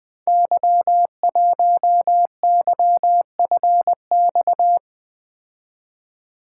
Morse code, heard as a keyed tone, about 20 wpm, 700 Hz. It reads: Y1YFX
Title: Code Groups - 5 Letter:Numbers - 20wpm